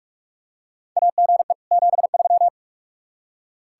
Morse code, heard as a keyed tone, about 46 wpm, 700 Hz. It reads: AGE 73
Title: Practice QSO - Exchange 3 - More Info - 45wpm